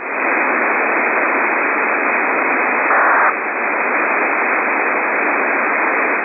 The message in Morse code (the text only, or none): none